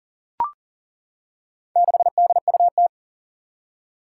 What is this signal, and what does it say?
Morse code, heard as a keyed tone, about 40 wpm, 700 Hz. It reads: 6DUT
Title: Four Challenging Letter-Number Combinations 40wpm